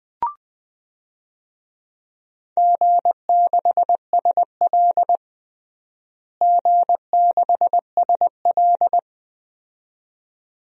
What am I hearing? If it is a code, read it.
Morse code, heard as a keyed tone, about 20 wpm, 700 Hz. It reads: G6SL G6SL
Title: DXCC-100 Call Signs - 4 Characters - Repeated 2x - 20wpm